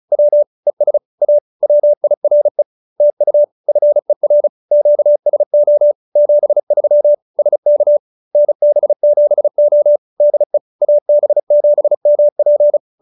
Morse code, heard as a keyed tone, about 35 wpm, 590 Hz. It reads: W ES A WIRE TU FER QSO 73 SK NB7O DE AB7MP